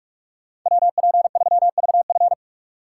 Morse code, heard as a keyed tone, about 45 wpm, 700 Hz. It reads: WP3VF